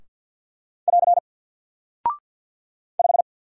Morse code, heard as a keyed tone, about 50 wpm, 700 Hz. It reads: <AR> 5
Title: Single Character (Letter:Number:Punctuation:Prosign) Alt 50wpm